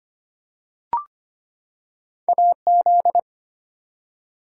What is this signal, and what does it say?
Morse code, heard as a keyed tone, about 25 wpm, 700 Hz. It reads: AZ
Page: morse-code-ninja-practice-25wpm